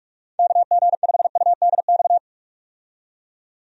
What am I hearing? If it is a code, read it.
Morse code, heard as a keyed tone, about 45 wpm, 700 Hz. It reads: KG5UDX
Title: USA Call Signs - 6 Characters - 45wpm